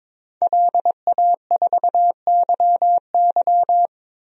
Morse code, heard as a keyed tone, about 22 wpm, 700 Hz. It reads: LA4YY
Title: DXCC-100 Call Signs - 5 Characters - Repeated 2x - 22wpm